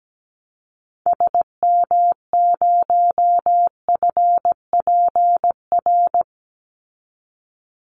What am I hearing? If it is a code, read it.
Morse code, heard as a keyed tone, about 17 wpm, 700 Hz. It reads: SM0FPR
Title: Call Signs Repeated 3x - 17wpm